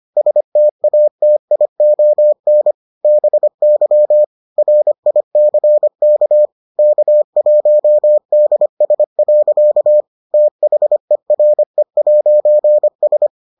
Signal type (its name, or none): Morse code